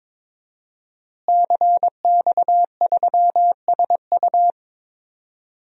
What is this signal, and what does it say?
Morse code, heard as a keyed tone, about 22 wpm, 700 Hz. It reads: CX3SU
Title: DXCC-100 Call Signs - 5 Characters - Repeated 3x - 22wpm